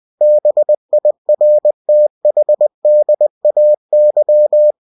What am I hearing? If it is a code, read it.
Morse code, heard as a keyed tone, about 20 wpm, 600 Hz. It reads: BIRTHDAY
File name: cw-books / winnie-the-pooh / winnie-20wpm-page-0050